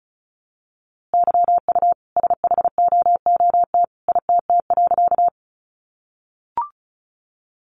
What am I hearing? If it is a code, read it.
Morse code, heard as a keyed tone, about 34 wpm, 700 Hz. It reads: YU SHOOT ITT.
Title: A Sound Of Thunder 35wpm